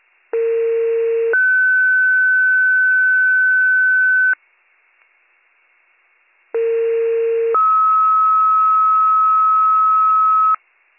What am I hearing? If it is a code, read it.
Two-tone page, heard as a one-second tone, then a three-second tone, about 460 then 1500 Hz, about 460 then 1290 Hz. It is heard from 0.3 to 10.6 s.